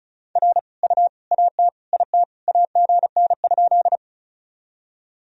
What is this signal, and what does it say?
Morse code, heard as a keyed tone, about 36 wpm, 700 Hz. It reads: R U AT IT AGN?
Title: Sentences from Top 200 Words and Abbreviations 35wpm